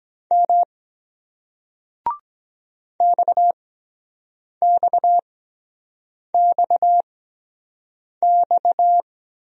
Morse code, heard as a keyed tone, about 23 wpm, spaced as for 7 wpm, 700 Hz. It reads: M XXXX